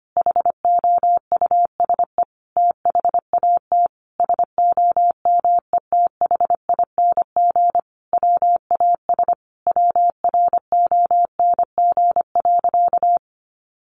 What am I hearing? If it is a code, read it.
Morse code, heard as a keyed tone, about 25 wpm, 700 Hz. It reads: HOUSE THAT SOMETHING WAS WRONG.
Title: Sentences from Top 1200 Words 25wpm